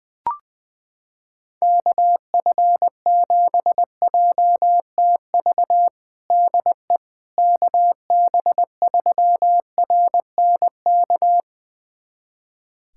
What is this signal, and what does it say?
Morse code, heard as a keyed tone, about 20 wpm, 700 Hz. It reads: KF7JTV DE KB3RNK
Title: USA Callsign DE Callsign 20wpm